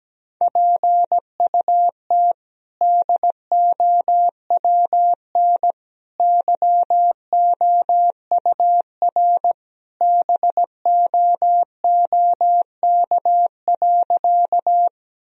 Morse code, heard as a keyed tone, about 17 wpm, 700 Hz. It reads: PUT DOWN YOUR BOOK.
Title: Sentences from Top 1000 Words - Review Part 1 - 17wpm